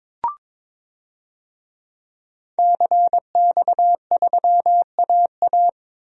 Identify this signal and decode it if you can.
Morse code, heard as a keyed tone, about 22 wpm, 700 Hz. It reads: CX3AA
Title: DXCC-100 Call Signs - 5 Characters - 22wpm